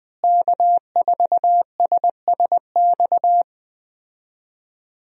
Morse code, heard as a keyed tone, about 21 wpm, 700 Hz. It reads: K4SSX
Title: USA Call Signs - 5 Characters - Repeated 3x - 20wpm